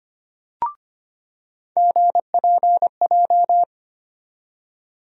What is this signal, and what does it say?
Morse code, heard as a keyed tone, about 25 wpm, 700 Hz. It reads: GPJ
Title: USA Call Sign Postfixes - 3 Letters 25wpm